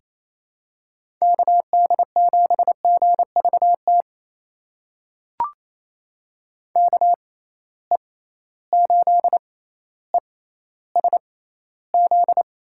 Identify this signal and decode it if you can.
Morse code, heard as a keyed tone, about 28 wpm, spaced as for 14 wpm, 700 Hz. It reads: KD7GVT KE8ESZ
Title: USA Call Signs - 6 Characters - 6x Character Spacing - 28wpm